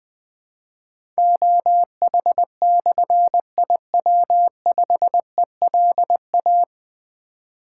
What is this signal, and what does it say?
Morse code, heard as a keyed tone, about 20 wpm, 700 Hz. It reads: OH/IW5ELA
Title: Difficult Call Signs 20wpm